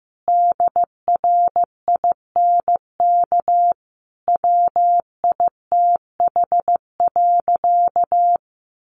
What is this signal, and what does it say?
Morse code, heard as a keyed tone, about 15 wpm, 700 Hz. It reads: DRINK WITH.
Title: Sentences from Top 1500 Words 15wpm